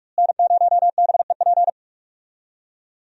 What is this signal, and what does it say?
Morse code, heard as a keyed tone, about 46 wpm, 700 Hz. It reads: N0BEP
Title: USA Call Signs - 5 Characters - Repeated 3x - 45wpm